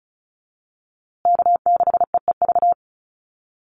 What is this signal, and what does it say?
Morse code, heard as a keyed tone, about 34 wpm, 700 Hz. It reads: K6EEV
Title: USA Call Signs Repeated 2x 35wpm